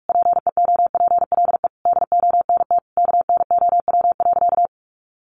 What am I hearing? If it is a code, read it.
Morse code, heard as a keyed tone, about 44 wpm, 700 Hz. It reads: PEOPLE DONT KNOW.
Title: Sentences from Top 600 Words 45wpm